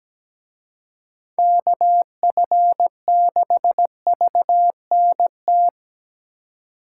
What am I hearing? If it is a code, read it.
Morse code, heard as a keyed tone, about 20 wpm, 700 Hz. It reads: KF6VNT
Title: Speed Racing - USA 2x3 Call Signs 17wpm